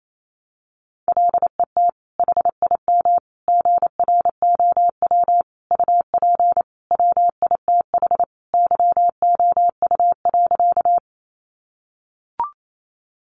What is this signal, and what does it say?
Morse code, heard as a keyed tone, about 28 wpm, 700 Hz. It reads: LET HIM GROW UP WITH YOU.
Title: Sentences from Top 300 Words 28wpm